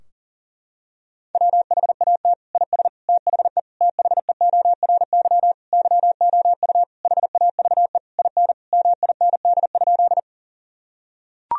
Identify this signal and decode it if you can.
Morse code, heard as a keyed tone, about 37 wpm, 700 Hz. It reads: WHAT IS THE THEORY YOU HAVE IN MIND?
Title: Sentences from Top 1000 Words - Review Part 3 - 40wpm